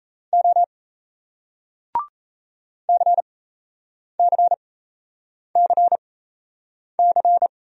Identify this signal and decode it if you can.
Morse code, heard as a keyed tone, about 37 wpm, spaced as for 8 wpm, 700 Hz. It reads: O CCCC